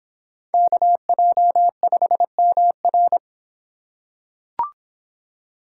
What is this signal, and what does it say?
Morse code, heard as a keyed tone, about 24 wpm, 700 Hz. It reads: KJ5MR